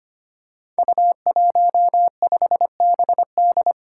Morse code, heard as a keyed tone, about 25 wpm, 700 Hz. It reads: U15BD
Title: Five Challenging Letter-Number Combinations 25wpm